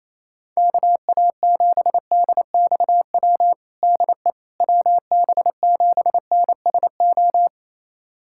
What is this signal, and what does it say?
Morse code, heard as a keyed tone, about 28 wpm, 700 Hz. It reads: KA7DXW DE WB7NSO
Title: USA Callsign DE Callsign 28wpm